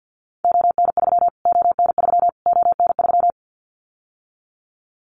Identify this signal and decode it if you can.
Morse code, heard as a keyed tone, about 50 wpm, 700 Hz. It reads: ON3 ON3 ON3
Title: Callsign Prefixes for Top 25 DXCC - Repeated 3x - 50wpm